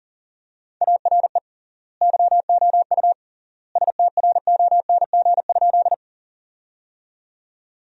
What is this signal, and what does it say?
Morse code, heard as a keyed tone, about 40 wpm, 700 Hz. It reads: ARE YOU STRONG?
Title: 3x Word-Spacing - Sentences from Top 500 Words - 40wpm